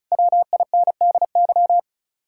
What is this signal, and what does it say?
Morse code, heard as a keyed tone, about 44 wpm, 700 Hz. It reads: WINDY